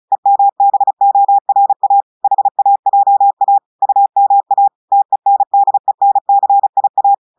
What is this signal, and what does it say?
Morse code, heard as a keyed tone, about 36 wpm, 800 Hz. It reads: EMBORA HAJA UMA TENDENCIA
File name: audios / portugais / portugais texte en clair 2 vitesse 35 mots minutes